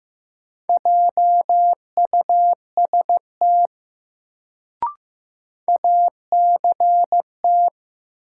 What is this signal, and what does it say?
Morse code, heard as a keyed tone, about 15 wpm, 700 Hz. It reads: JUST ACT
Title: Sentences from Top 200 Words - New Word Drill - 15wpm